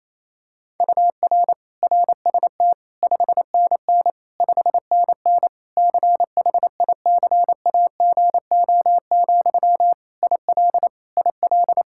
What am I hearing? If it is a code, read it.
Morse code, heard as a keyed tone, about 28 wpm, 700 Hz. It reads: UR RST 5NN 5NN CHICAGO, IL IL
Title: K1USN SST QSO Practice 28wpm